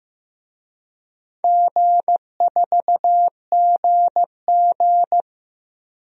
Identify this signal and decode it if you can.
Morse code, heard as a keyed tone, about 15 wpm, 700 Hz. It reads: G4GG
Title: DXCC-100 Call Signs - 4 Characters - 15wpm